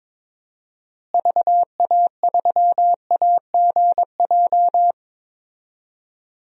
Morse code, heard as a keyed tone, about 21 wpm, 700 Hz. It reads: VA3AGJ
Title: DXCC-100 Call Signs - 6 Characters - Repeated 3x - 22wpm